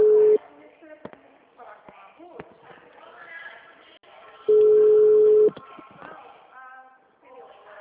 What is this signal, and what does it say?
Ringback tone (Europe, 425 Hz), with speech.